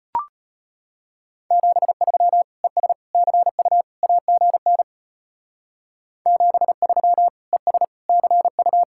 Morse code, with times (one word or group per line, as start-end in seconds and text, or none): 1.5-2.4 73
2.6-2.9 ES
3.1-3.8 CU
4.0-4.8 AGN
6.3-7.3 73
7.5-7.8 ES
8.1-8.8 CU